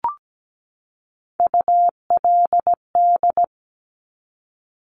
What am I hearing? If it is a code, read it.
Morse code, heard as a keyed tone, about 17 wpm, 700 Hz. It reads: ULD
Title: Top Three Letter Combinations in Words 17wpm